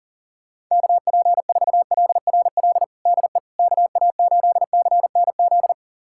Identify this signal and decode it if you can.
Morse code, heard as a keyed tone, about 40 wpm, 700 Hz. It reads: KP4LRL DE KA8CNZ
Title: USA Callsign DE Callsign 40wpm